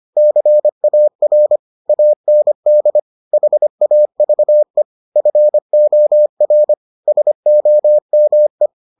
Morse code, heard as a keyed tone, about 25 wpm, 600 Hz. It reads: CAR AND HAVE FOR SOME